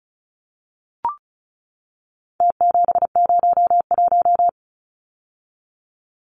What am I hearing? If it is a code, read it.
Morse code, heard as a keyed tone, about 34 wpm, 700 Hz. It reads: T701
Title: Numbers - Contest Style - 4-Digit 35wpm